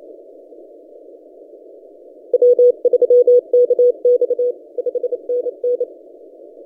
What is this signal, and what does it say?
Morse code, heard as a keyed tone, about 30 wpm, 490 Hz. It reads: W3KX 5NN
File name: t30d-12cw